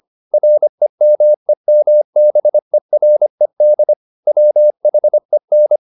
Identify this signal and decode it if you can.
Morse code, heard as a keyed tone, about 25 wpm, 600 Hz. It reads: REMEMBERED WHEN
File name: the old man and the sea part 1 25wpm